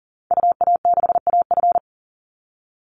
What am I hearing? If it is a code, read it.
Morse code, heard as a keyed tone, about 40 wpm, 700 Hz. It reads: UA6AF
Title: Call Signs Repeated 2x - 40wpm